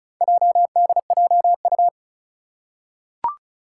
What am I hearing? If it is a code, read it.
Morse code, heard as a keyed tone, about 36 wpm, 700 Hz. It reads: JDJU
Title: Four Challenging Letter-Number Combinations 35wpm